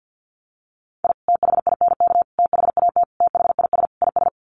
Morse code, heard as a keyed tone, about 50 wpm, 700 Hz. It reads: I THINK THAT THIS IS
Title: Sets of 5 Words Encore2 50wpm